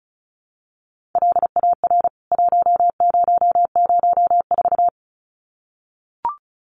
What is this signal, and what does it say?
Morse code, heard as a keyed tone, about 34 wpm, 700 Hz. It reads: LAR 1004
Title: CWT Exchange 35wpm